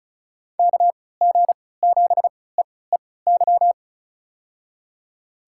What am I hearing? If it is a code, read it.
Morse code, heard as a keyed tone, about 34 wpm, spaced as for 20 wpm, 700 Hz. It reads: KG7EEY
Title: USA Call Signs - 3x Character Spacing - 35wpm